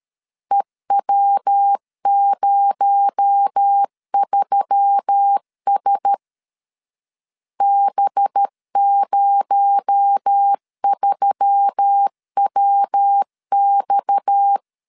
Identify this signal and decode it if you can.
Morse code, heard as a keyed tone, about 13 wpm, 800 Hz. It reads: EW03S B03WX
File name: LEZIONI / 6 PKX0 / 6 PKX0 12wpm 2231